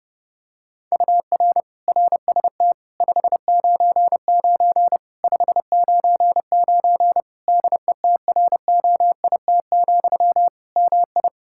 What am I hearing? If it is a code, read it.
Morse code, heard as a keyed tone, about 30 wpm, 700 Hz. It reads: UR RST 599 599 DETROIT, MI
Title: Practice QSO - Exchange 1 - Beginning - 30wpm